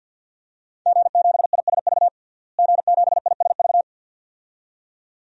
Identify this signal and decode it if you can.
Morse code, heard as a keyed tone, about 51 wpm, 700 Hz. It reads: G7ISV G7ISV
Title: DXCC-100 Call Signs - 5 Characters - Repeated 2x - 50wpm